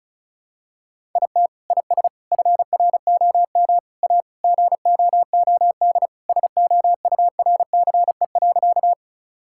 Morse code, heard as a keyed tone, about 34 wpm, 700 Hz. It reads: IT IS FROM A GOOD SOURCE.
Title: Sentences from Top 1000 Words - Review Part 5 - 35wpm